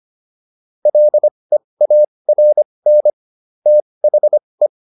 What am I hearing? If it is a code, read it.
Morse code, heard as a keyed tone, about 25 wpm, spaced as for 20 wpm, 600 Hz. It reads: LEARN THE